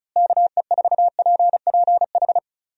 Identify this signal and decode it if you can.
Morse code, heard as a keyed tone, about 36 wpm, 700 Hz. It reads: KE4PPH
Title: USA Call Signs - 6 Characters - Repeated 3x - 35wpm